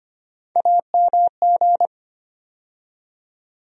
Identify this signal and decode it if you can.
Morse code, heard as a keyed tone, about 25 wpm, 700 Hz. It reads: AMG
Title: USA Call Sign Postfixes - 3 Letters 25wpm